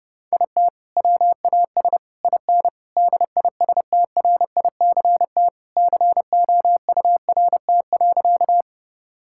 Morse code, heard as a keyed tone, about 30 wpm, 700 Hz. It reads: IT WAS IN DISTRICT COURT.
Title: Sentences from Top 1600 Words 30wpm